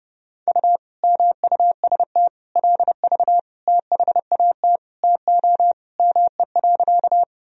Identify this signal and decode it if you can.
Morse code, heard as a keyed tone, about 30 wpm, 700 Hz. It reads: U MUST LV THAT TO ME.
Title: Sentences from Top 400 Words and Abbreviations 30wpm